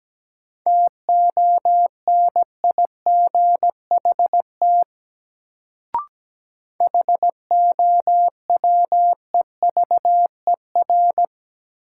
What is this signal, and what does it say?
Morse code, heard as a keyed tone, about 17 wpm, 700 Hz. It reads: TONIGHT HOWEVER
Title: Sentences from Top 600 Words - New Word Drill - 17wpm